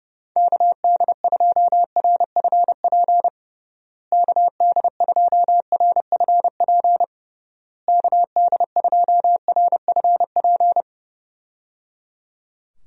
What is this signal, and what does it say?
Morse code, heard as a keyed tone, about 29 wpm, 700 Hz. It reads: KD2RFP KD2RFP KD2RFP
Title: USA Call Signs - 6 Characters - Repeated 3x - 30wpm